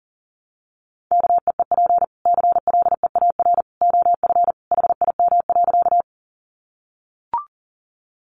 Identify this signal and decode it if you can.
Morse code, heard as a keyed tone, about 40 wpm, 700 Hz. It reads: KEEP CLEAR OF HIM.
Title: Sentences from Top 500 Words 40wpm